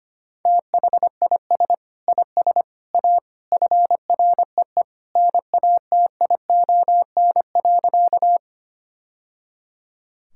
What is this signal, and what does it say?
Morse code, heard as a keyed tone, about 25 wpm, 700 Hz. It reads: THIS IS A FREE NATION.
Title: Sentences from Top 1000 Words - Review Part 6 - 25wpm